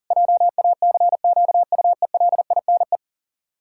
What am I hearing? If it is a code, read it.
Morse code, heard as a keyed tone, about 37 wpm, 700 Hz. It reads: JACQUELINE
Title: Practice QSO - Exchange 1 - Beginning - 40wpm